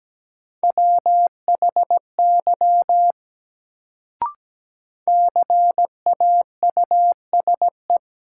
Morse code, heard as a keyed tone, about 17 wpm, 700 Hz. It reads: WHY CAUSE
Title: Sentences from Top 200 Words - New Word Drill - 17wpm